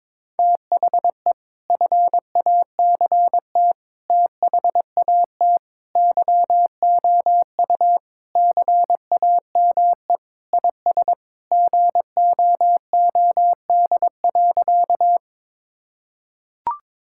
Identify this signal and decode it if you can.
Morse code, heard as a keyed tone, about 22 wpm, 700 Hz. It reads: THE FACT THAT YOU CAME IS GOOD.
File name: practice / 22wpm / Sentences with 3-Word Phrases - 22wpm